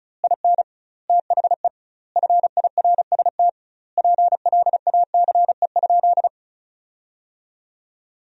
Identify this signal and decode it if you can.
Morse code, heard as a keyed tone, about 34 wpm, 700 Hz. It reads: IN THE FIRST PLACE?
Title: 2x Word-Spacing - Sentences from Top 200 Words - 35wpm